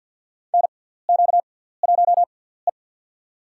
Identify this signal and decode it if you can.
Morse code, heard as a keyed tone, about 48 wpm, spaced as for 17 wpm, 700 Hz. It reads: NQ1E
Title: USA Call Signs - 4 Characters - 6x Character Spacing - 50wpm